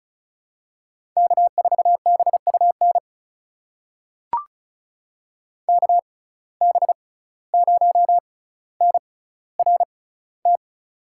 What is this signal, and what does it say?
Morse code, heard as a keyed tone, about 36 wpm, spaced as for 17 wpm, 700 Hz. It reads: K4BUN KB0NRT
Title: USA Call Signs - 6x Character Spacing - 35wpm